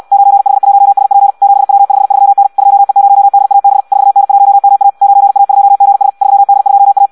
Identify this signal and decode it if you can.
Morse code, heard as a keyed tone, about 74 wpm, 800 Hz. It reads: 9H1SP 7W53N 2E0DAB 4N1MA 9A2GS 3D2A